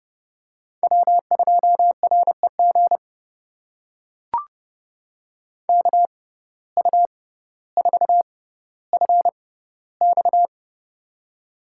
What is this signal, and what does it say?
Morse code, heard as a keyed tone, about 30 wpm, spaced as for 15 wpm, 700 Hz. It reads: W2REG KU4FX
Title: USA Call Signs - 5 Characters - 6x Character Spacing - 30wpm